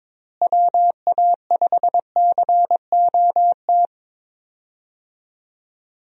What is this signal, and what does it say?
Morse code, heard as a keyed tone, about 22 wpm, 700 Hz. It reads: WA5COT
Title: USA Call Signs - 6 Characters - Repeated 3x - 22wpm